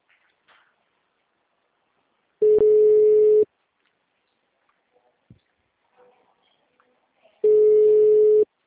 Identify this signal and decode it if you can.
Ringback tone (Europe, 425 Hz).
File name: out- -2031- - - .52297